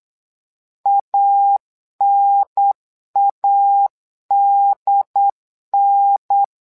Morse code, heard as a keyed tone, about 8 wpm, 800 Hz. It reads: ANADN
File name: LEZIONI / 2 UHDN / 2 UHDN 8wpm2232